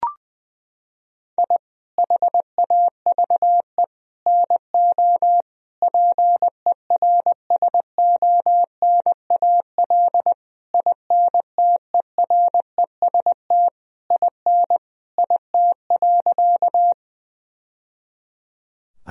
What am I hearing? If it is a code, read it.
Morse code, heard as a keyed tone, about 20 wpm, 700 Hz. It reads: I HAVE NO PERSONAL INTEREST IN IT.
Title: Sentences from Top 1000 Words - Review Part 4 - 20wpm